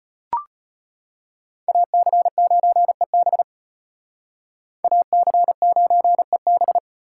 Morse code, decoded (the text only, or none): AC9EB AC9EB